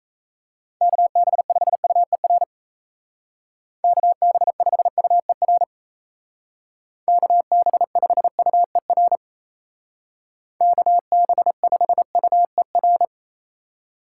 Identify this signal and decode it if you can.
Morse code, heard as a keyed tone, about 36 wpm, 700 Hz. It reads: KB5UER KB5UER KB5UER KB5UER